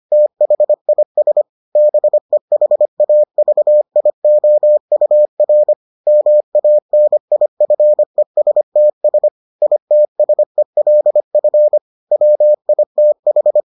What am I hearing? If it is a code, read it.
Morse code, heard as a keyed tone, about 25 wpm, 600 Hz. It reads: THIS BEHAVIOUR MANIFESTS ITSELF WITH